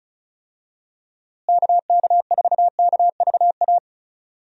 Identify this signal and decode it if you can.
Morse code, heard as a keyed tone, about 36 wpm, 700 Hz. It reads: KK4KVA
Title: USA Call Signs - 6 Characters - Repeated 2x - 35wpm